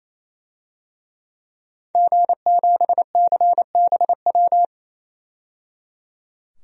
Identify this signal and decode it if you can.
Morse code, heard as a keyed tone, about 28 wpm, 700 Hz. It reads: G7CBW
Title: DXCC-100 Call Signs - 5 Characters - 28wpm